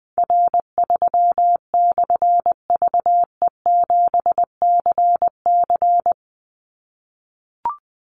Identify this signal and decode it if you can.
Morse code, heard as a keyed tone, about 20 wpm, 700 Hz. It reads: R3/VE7CC
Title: Call Signs - 20wpm